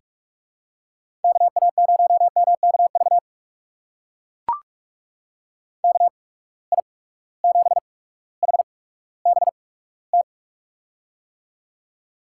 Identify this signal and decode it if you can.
Morse code, heard as a keyed tone, about 44 wpm, spaced as for 19 wpm, 700 Hz. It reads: KA0MKV KI7HBT
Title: USA Call Signs - 6 Characters - 8x Character Spacing - 45wpm